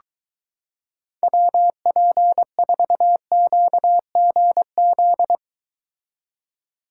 Morse code, heard as a keyed tone, about 22 wpm, 700 Hz. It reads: WP4QGZ